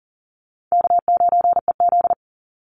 Morse code, heard as a keyed tone, about 41 wpm, 700 Hz. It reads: K9EZ
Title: NA SOTA Chaser Callsigns 40wpm